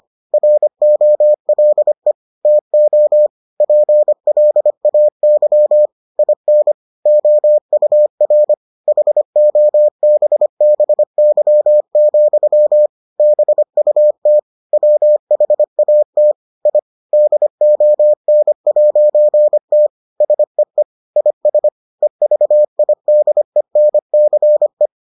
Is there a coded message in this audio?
Yes: Morse code from 0.3 to 24.9 s, heard as a keyed tone, about 25 wpm, 600 Hz. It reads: ROLE TO PLAY IN OUR HOBBY, BUT WHAT I DON'T SEE IS EVIDENCE